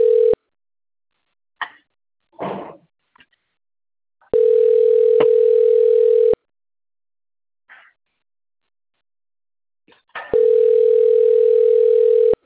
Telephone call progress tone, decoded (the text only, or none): ringback tone